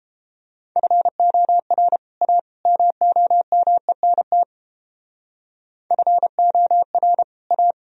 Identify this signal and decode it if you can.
Morse code, heard as a keyed tone, about 29 wpm, 700 Hz. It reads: FOR A MOMENT FOR A